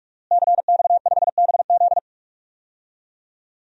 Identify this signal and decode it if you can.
Morse code, heard as a keyed tone, about 41 wpm, 700 Hz. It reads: CX5BZ
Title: DXCC-100 Call Signs - 5 Characters - Repeated 2x - 45wpm